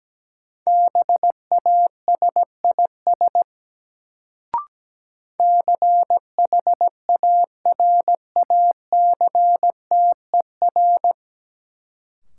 Morse code, heard as a keyed tone, about 17 wpm, 700 Hz. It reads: BASIS CHARACTER
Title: Sentences from Top 1200 Words - New Word Drill Alt - 17wpm